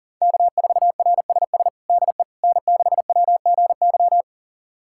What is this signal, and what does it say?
Morse code, heard as a keyed tone, about 37 wpm, 700 Hz. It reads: K4RSS DE N6WGY
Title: USA Callsign DE Callsign 40wpm